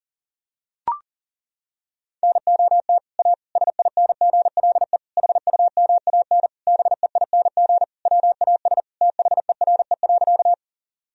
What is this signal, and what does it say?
Morse code, heard as a keyed tone, about 40 wpm, 700 Hz. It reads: NOT A SINGLE HUMAN BEING WAS THERE.
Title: Sentences from Top 1000 Words - Review Part 4 - 40wpm